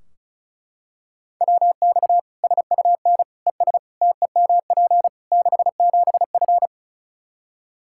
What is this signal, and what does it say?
Morse code, heard as a keyed tone, about 34 wpm, 700 Hz. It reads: WX SUN ES TEMP 67F
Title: Practice QSO - Exchange 2 - Info - 35wpm